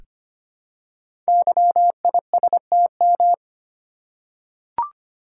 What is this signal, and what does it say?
Morse code, heard as a keyed tone, about 25 wpm, 700 Hz. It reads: YISTM
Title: Code Groups - 5 Letters - 25wpm